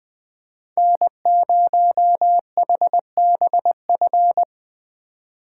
Morse code, heard as a keyed tone, about 20 wpm, 700 Hz. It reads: N0HBF